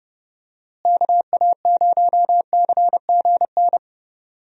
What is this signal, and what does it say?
Morse code, heard as a keyed tone, about 30 wpm, 700 Hz. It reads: KA0CGN
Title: USA Call Signs - 6 Characters - Repeated 3x - 30wpm